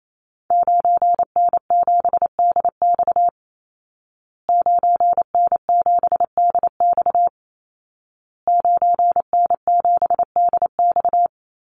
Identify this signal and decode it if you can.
Morse code, heard as a keyed tone, about 28 wpm, 700 Hz. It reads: 9N7DX 9N7DX 9N7DX
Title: Call Signs Repeated 2x - 28wpm